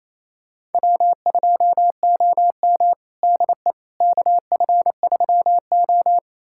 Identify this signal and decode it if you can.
Morse code, heard as a keyed tone, about 28 wpm, 700 Hz. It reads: W2OM DE KF3O
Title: USA Callsign DE Callsign - 4 Characters - 28wpm